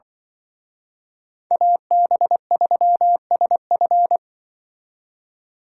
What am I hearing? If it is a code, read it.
Morse code, heard as a keyed tone, about 24 wpm, 700 Hz. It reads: AB3SF
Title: Speed Racing - USA 2x2 Call Signs 20wpm